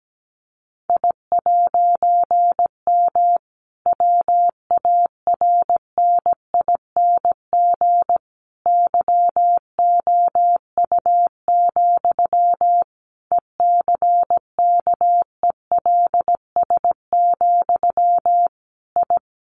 Morse code, heard as a keyed tone, about 17 wpm, 700 Hz. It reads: I'M WARNING YOU, ECKELS, I
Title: A Sound Of Thunder 17wpm